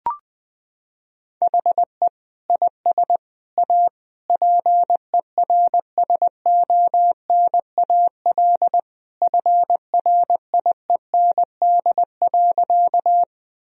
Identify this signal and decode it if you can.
Morse code, heard as a keyed tone, about 20 wpm, 700 Hz. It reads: HE IS A PERSONAL FRIEND.
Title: Sentences from Top 1000 Words - Review Part 4 - 20wpm